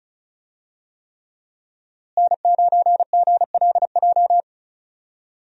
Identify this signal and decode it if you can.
Morse code, heard as a keyed tone, about 36 wpm, 700 Hz. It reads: N9GLJ
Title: USA Call Signs - 5 Characters - Repeated 3x - 35wpm